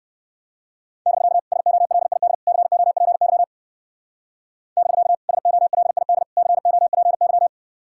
Morse code, heard as a keyed tone, about 55 wpm, 700 Hz. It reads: <BK> SOLID COPY <BK> SOLID COPY
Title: Speed-Racing Common QSO Phrases - 45wpm